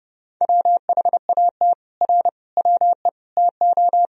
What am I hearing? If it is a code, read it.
Morse code, heard as a keyed tone, about 30 wpm, 700 Hz. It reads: WHAT R WE TO
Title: Sentences from Top 100 Words and Abbreviations 30wpm